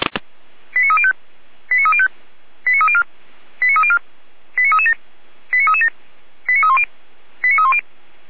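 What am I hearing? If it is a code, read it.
Five-tone selective call (ZVEI), heard as quick runs of five tones, tones of about 80 ms. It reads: 89305 89305 89304 89304 89337 89337 89310 89310
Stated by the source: ZVEI five-tone call 89270 received by SDR.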